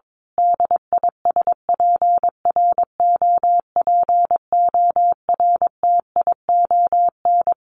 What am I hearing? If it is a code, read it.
Morse code, heard as a keyed tone, about 22 wpm, 700 Hz. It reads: DISPROPORTION
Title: A Sound Of Thunder 22wpm